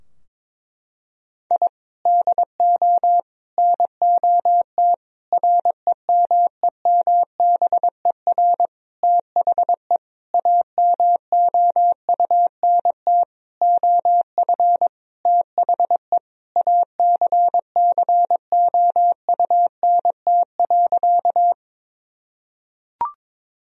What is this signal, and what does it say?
Morse code, heard as a keyed tone, about 22 wpm, 700 Hz. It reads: I DO NOT REMEMBER THE AMOUNT OF THE ACCOUNT.
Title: Sentences from Top 1000 Words - Review Part 4 - 22wpm